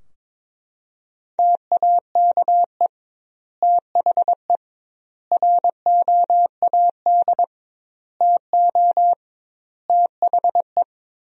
Morse code, heard as a keyed tone, about 21 wpm, 700 Hz. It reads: TAKE THE ROAD TO THE
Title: 2x Word-Spacing - Sentences from Top 400 Words - 22wpm